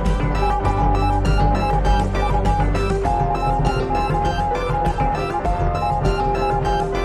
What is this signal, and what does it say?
Morse code, heard as a keyed tone, about 19 wpm, 820 Hz. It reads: YOU COME TO